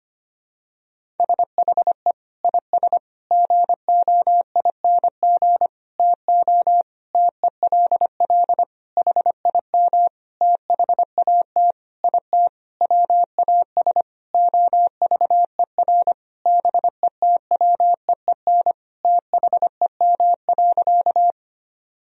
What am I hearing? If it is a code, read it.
Morse code, heard as a keyed tone, about 25 wpm, 700 Hz. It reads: SHE IS GOING TO TELL HIM THAT IT WAS OVER BETWEEN THEM.
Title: Sentences from Top 1000 Words - Review Part 1 - 25wpm